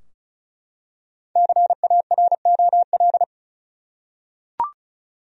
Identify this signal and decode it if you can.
Morse code, heard as a keyed tone, about 36 wpm, 700 Hz. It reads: CAROL
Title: USA Common Names - 35wpm